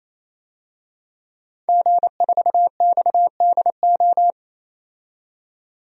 Morse code, heard as a keyed tone, about 28 wpm, 700 Hz. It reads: G4XDO
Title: DXCC-100 Call Signs - 5 Characters - 28wpm